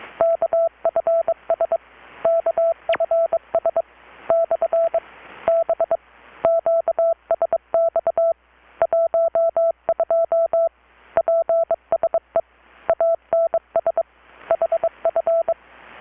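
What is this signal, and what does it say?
Morse code, heard as a keyed tone, about 23 wpm, 660 Hz. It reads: KFS KFS / B QSX 12 PSE ANS HF
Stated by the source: Morse from KPH's Night of Nights XX maritime radiotelegraphy event, received in Chiapas, Mexico about 3,800 km away on a Kenwood R-600 with a grounded 10 m long-wire antenna.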